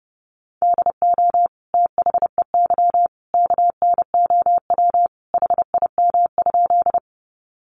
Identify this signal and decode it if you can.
Morse code, heard as a keyed tone, about 30 wpm, 700 Hz. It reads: DO THEY KNOW HIM?
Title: Sentences from Top 100 Words 30wpm